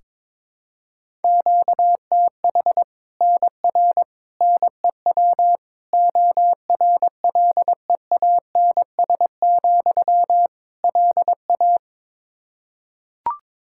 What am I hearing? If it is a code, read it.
Morse code, heard as a keyed tone, about 22 wpm, 700 Hz. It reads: QTH NR NEW ORLEANS, LA
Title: Practice QSO - Exchange 1 - Beginning - 22wpm